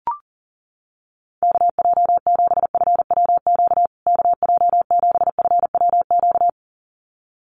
Morse code, heard as a keyed tone, about 40 wpm, 700 Hz. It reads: KJ7FWQ KJ7FWQ
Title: USA Call Signs Repeated 2x 40wpm